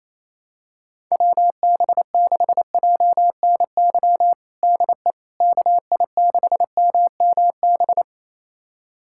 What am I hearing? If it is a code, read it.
Morse code, heard as a keyed tone, about 28 wpm, 700 Hz. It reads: WB6JNY DE KI6MMB
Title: USA Callsign DE Callsign 28wpm